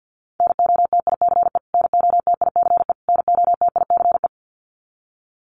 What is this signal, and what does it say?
Morse code, heard as a keyed tone, about 49 wpm, 700 Hz. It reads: NOTICE NOTICE NOTICE
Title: Top 500 Words Repeated 50wpm